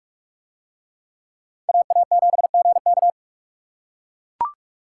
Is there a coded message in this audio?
Yes: Morse code from 1.7 to 3.1 s, heard as a keyed tone, about 42 wpm, 700 Hz. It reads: AA7GK